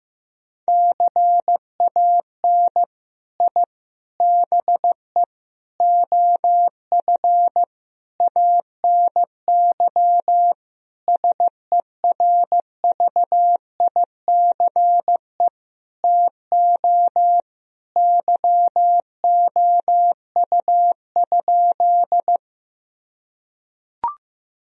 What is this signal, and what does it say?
Morse code, heard as a keyed tone, about 15 wpm, 700 Hz. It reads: CAN I BE OF ANY SERVICE TO YOU?
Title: Sentences from Top 1000 Words - Review Part 4 - 15wpm